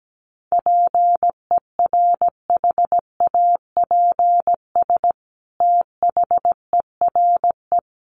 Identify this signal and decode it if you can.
Morse code, heard as a keyed tone, about 17 wpm, 700 Hz. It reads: PERHAPS THERE
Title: Sentences from Top 600 Words 17wpm